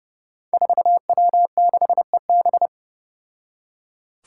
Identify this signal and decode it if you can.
Morse code, heard as a keyed tone, about 30 wpm, 700 Hz. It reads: 4W6EB
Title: Five Challenging Letter-Number Combinations 30wpm